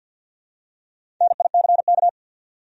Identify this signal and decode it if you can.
Morse code, heard as a keyed tone, about 51 wpm, 700 Hz. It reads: NICK
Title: USA Common Names - Max 4 Letters - 50wpm